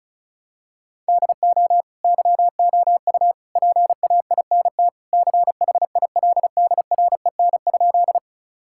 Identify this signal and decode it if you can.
Morse code, heard as a keyed tone, about 36 wpm, 700 Hz. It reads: DO YOU PAINT CHILDREN?
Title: Sentences from Top 1000 Words - Review Part 6 - 35wpm